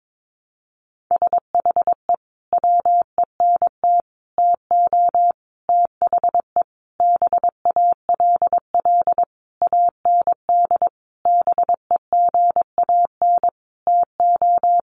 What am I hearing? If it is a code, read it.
Morse code, heard as a keyed tone, about 22 wpm, 700 Hz. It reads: SHE WENT TO THE BALL AND BEGAN TO
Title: Sentences from Top 900 Words 22wpm